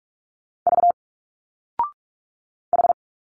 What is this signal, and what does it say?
Morse code, heard as a keyed tone, about 45 wpm, 700 Hz. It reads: V H
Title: Single Letters 45wpm